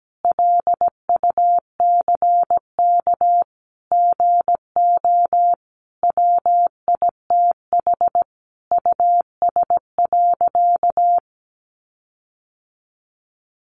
Morse code, heard as a keyed tone, about 17 wpm, 700 Hz. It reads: LUCK GO WITH US.
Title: Sentences from Top 1500 Words 17wpm